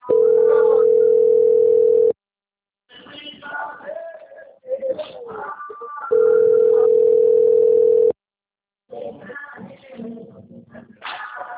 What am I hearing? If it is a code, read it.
Ringback tone (North America), 440 + 480 Hz.
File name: out- - - - - .28257